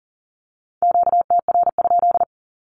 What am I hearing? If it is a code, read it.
Morse code, heard as a keyed tone, about 40 wpm, 700 Hz. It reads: QTR?
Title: Q-Codes 40wpm